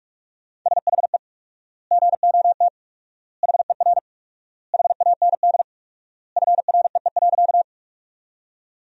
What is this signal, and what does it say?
Morse code, heard as a keyed tone, about 46 wpm, 700 Hz. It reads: SHE GOT HER HAND FREE.
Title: 4x Word-Spacing - Sentences from Top 500 Words - 45wpm